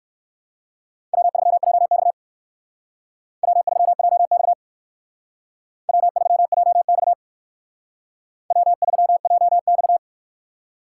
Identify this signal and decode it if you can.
Morse code, heard as a keyed tone, about 57 wpm, 700 Hz. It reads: W3JX W3JX W3JX W3JX